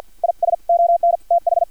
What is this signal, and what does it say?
Morse code, heard as a keyed tone, about 50 wpm, 690 Hz. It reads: IS ON TH
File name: archived / CRSnet